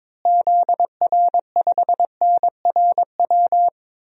Morse code, heard as a keyed tone, about 22 wpm, 700 Hz. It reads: ZR5NRW
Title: DXCC-100 Call Signs - 6 Characters - Repeated 3x - 22wpm